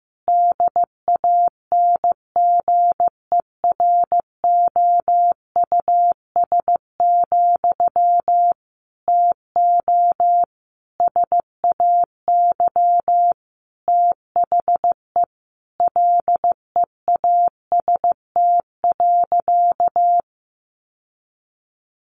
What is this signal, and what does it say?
Morse code, heard as a keyed tone, about 15 wpm, 700 Hz. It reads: DANGEROUS, TO SAY THE LEAST.
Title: Sentences from Top 1200 Words 15wpm